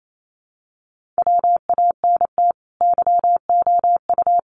Morse code, heard as a keyed tone, about 28 wpm, 700 Hz. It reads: WANT YOU
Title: Sets of 5 Words Encore 28wpm